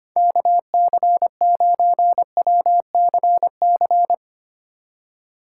Morse code, heard as a keyed tone, about 31 wpm, 700 Hz. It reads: KC9WCC